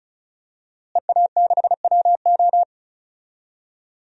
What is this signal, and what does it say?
Morse code, heard as a keyed tone, about 36 wpm, 700 Hz. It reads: EA6WO
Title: DXCC-100 Call Signs - 5 Characters - Repeated 3x - 35wpm